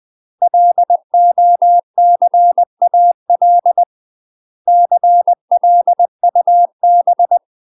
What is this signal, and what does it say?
Morse code, heard as a keyed tone, about 20 wpm, 700 Hz. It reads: LOCAL CLUB
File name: UK MASTER QSOs TRAINING / 20WPM / UK MASTER QSOs 0045